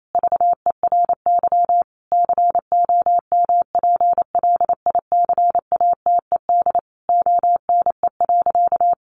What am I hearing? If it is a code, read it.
Morse code, heard as a keyed tone, about 28 wpm, 700 Hz. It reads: VERY COMPLICATED ONE.
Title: Sentences from Top 1200 Words 28wpm